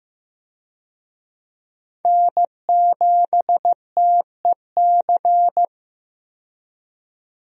Morse code, heard as a keyed tone, about 15 wpm, 700 Hz. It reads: N7TEC
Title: USA Call Signs - 5 Characters - 15wpm